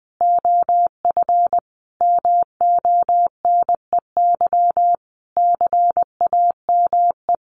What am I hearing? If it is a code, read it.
Morse code, heard as a keyed tone, about 20 wpm, 700 Hz. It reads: OF MONEY CAME
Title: Sentences from Top 1500 Words 20wpm